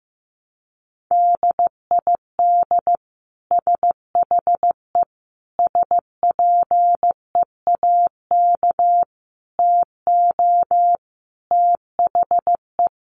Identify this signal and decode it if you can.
Morse code, heard as a keyed tone, about 15 wpm, 700 Hz. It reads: DID SHE SPEAK TO THE
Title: Sentences from Top 1200 Words 15wpm